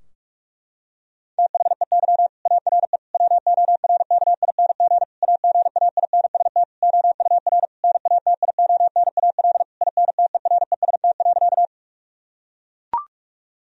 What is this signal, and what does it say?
Morse code, heard as a keyed tone, about 41 wpm, 700 Hz. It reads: THEY ARE WORKING AGAINST OUR NATIONAL INTEREST.
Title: Sentences from Top 1000 Words - Review Part 4 - 45wpm